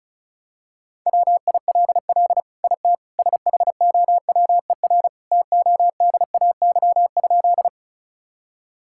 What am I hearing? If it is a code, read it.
Morse code, heard as a keyed tone, about 34 wpm, 700 Hz. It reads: WILL IT SHOWER TODAY?
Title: Sentences from Top 1600 Words 35wpm